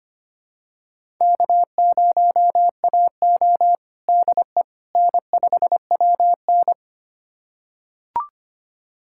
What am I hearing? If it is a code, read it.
Morse code, heard as a keyed tone, about 25 wpm, 700 Hz. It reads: K0AO DE N5WN
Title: USA Callsign DE Callsign - 4 Characters - 25wpm